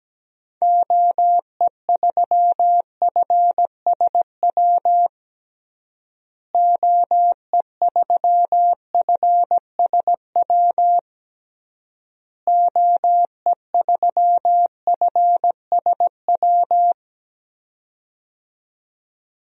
Morse code, heard as a keyed tone, about 17 wpm, 700 Hz. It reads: OE3FSW OE3FSW OE3FSW
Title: DXCC-100 Call Signs - 6 Characters - Repeated 3x - 17wpm